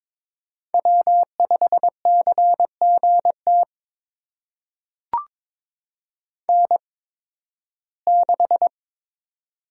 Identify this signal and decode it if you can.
Morse code, heard as a keyed tone, about 22 wpm, spaced as for 14 wpm, 700 Hz. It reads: W5CGT N6
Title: USA Call Signs - 5 Characters - 8x Character Spacing - 22wpm